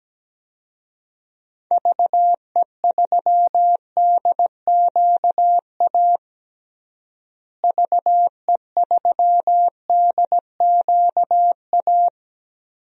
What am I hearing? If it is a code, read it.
Morse code, heard as a keyed tone, about 17 wpm, 700 Hz. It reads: VE3DQA VE3DQA
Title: DXCC-100 Call Signs - 6 Characters - Repeated 3x - 17wpm